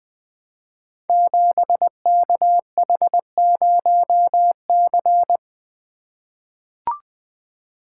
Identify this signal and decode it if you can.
Morse code, heard as a keyed tone, about 20 wpm, 700 Hz. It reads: 7KH0C
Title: Code Groups - 5 Letter:Numbers - 20wpm